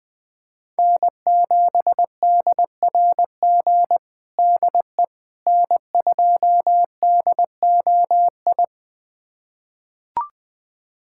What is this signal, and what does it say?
Morse code, heard as a keyed tone, about 20 wpm, 700 Hz. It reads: N7DRG DE N2DOI
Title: USA Callsign DE Callsign - 5 Characters - 20wpm